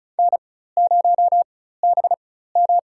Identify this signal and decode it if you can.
Morse code, heard as a keyed tone, about 36 wpm, spaced as for 16 wpm, 700 Hz. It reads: N0BM
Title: USA Call Signs - 4 Characters - 4x Character Spacing - 35wpm